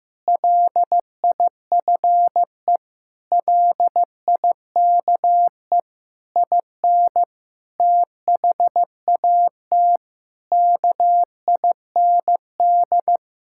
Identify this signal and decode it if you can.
Morse code, heard as a keyed tone, about 15 wpm, 700 Hz. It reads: LIFE LIKE IN THAT KIND
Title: Sentences from Top 1000 Words - Review Part 5 - 15wpm